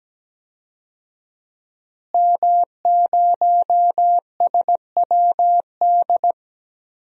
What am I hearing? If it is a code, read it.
Morse code, heard as a keyed tone, about 17 wpm, 700 Hz. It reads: M0SWD
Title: DXCC-100 Call Signs - 5 Characters - Repeated 2x - 17wpm